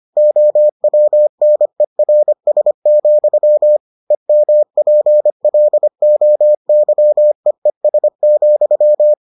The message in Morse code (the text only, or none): OWNERS, EMPLOYEES,